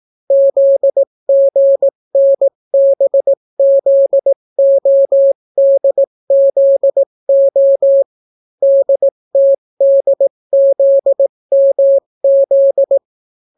Morse code, heard as a keyed tone, about 18 wpm, 550 Hz. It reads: ZGNBZODZO DTDZMZ